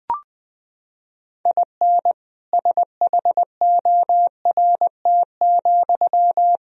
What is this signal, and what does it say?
Morse code, heard as a keyed tone, about 20 wpm, 700 Hz. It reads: IN SHORT,
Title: Sentences from Top 1000 Words - Review Part 2 - 20wpm